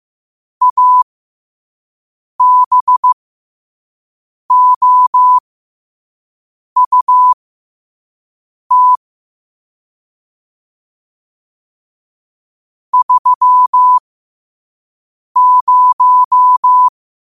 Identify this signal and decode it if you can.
Morse code, heard as a keyed tone, about 15 wpm, spaced as for 5 wpm, 1000 Hz. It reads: ABOUT 30
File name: qotdfiles / 5wpm